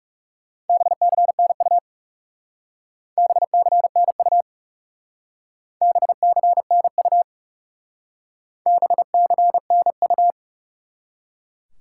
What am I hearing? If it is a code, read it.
Morse code, heard as a keyed tone, about 38 wpm, 700 Hz. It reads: BCNU BCNU BCNU BCNU
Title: Speed-Racing Top 180 QSO Elements 30wpm